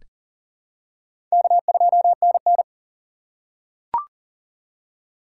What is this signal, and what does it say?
Morse code, heard as a keyed tone, about 37 wpm, 700 Hz. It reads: K2NN
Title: USA Call Signs - 4 Characters - 4x Character Spacing - 40wpm